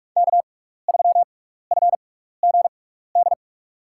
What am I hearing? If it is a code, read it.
Morse code, heard as a keyed tone, about 46 wpm, spaced as for 16 wpm, 700 Hz. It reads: K3FGD
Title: USA Call Signs - 5 Characters - 6x Character Spacing - 45wpm